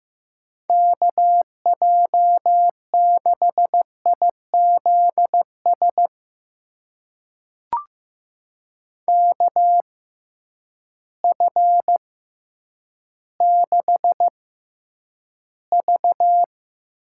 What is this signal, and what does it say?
Morse code, heard as a keyed tone, about 15 wpm, spaced as for 9 wpm, 700 Hz. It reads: KJ6IZS KF6V
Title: USA Call Signs - 6 Characters - 6x Character Spacing - 15wpm